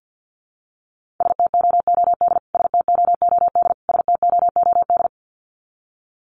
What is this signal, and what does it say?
Morse code, heard as a keyed tone, about 50 wpm, 700 Hz. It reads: STOOD STOOD STOOD
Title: Top 500 Words Repeated 50wpm